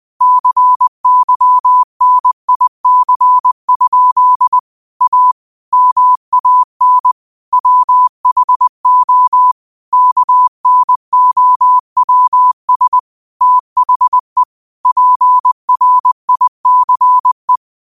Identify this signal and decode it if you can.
Morse code, heard as a keyed tone, about 20 wpm, 1000 Hz. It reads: CYNIC? A MAN WHO KNOWS THE PRICE
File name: qotdfiles / 20wpm